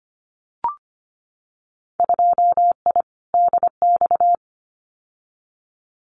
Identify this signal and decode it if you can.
Morse code, heard as a keyed tone, about 25 wpm, 700 Hz. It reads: 2I DX
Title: Winter Field Day Exchanges 25wpm